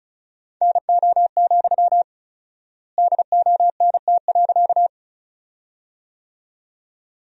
Morse code, heard as a keyed tone, about 34 wpm, 700 Hz. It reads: NO, DONT.
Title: 4x Word-Spacing - Sentences from Top 300 Words - 35wpm